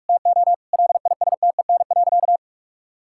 Morse code, heard as a keyed tone, about 41 wpm, 700 Hz. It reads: TO LISTEN.
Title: Sentences from Top 1000 Words - Review Part 6 - 45wpm